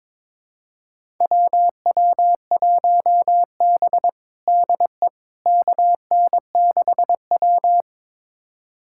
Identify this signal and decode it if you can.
Morse code, heard as a keyed tone, about 22 wpm, 700 Hz. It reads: WW1B DE KN6W
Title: USA Callsign DE Callsign - 4 Characters - 22wpm